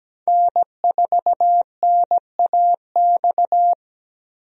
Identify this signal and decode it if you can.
Morse code, heard as a keyed tone, about 17 wpm, 700 Hz. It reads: N4NAX
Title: USA Call Signs - 5 Characters - Repeated 3x - 17wpm